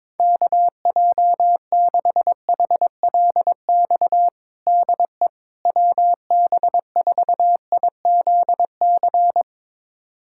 Morse code, heard as a keyed tone, about 22 wpm, 700 Hz. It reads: KJ6HLX DE WB4IZC
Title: USA Callsign DE Callsign - 6 Characters - 22wpm